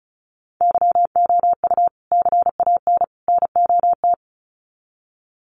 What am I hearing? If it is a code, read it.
Morse code, heard as a keyed tone, about 34 wpm, 700 Hz. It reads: YOU CAN NOT
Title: Sets of 3 Words 35wpm